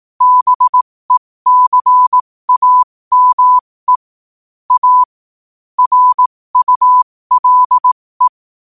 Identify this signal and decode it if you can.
Morse code, heard as a keyed tone, about 18 wpm, spaced as for 15 wpm, 1000 Hz. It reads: BECAME A RULE
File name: 15wpm